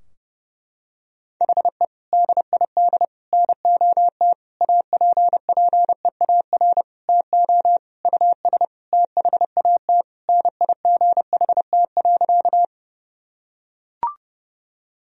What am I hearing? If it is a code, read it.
Morse code, heard as a keyed tone, about 30 wpm, 700 Hz. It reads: HE DID NOT APPEAR TO US THAT NIGHT.
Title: Sentences from Top 1000 Words - Review Part 2 - 30wpm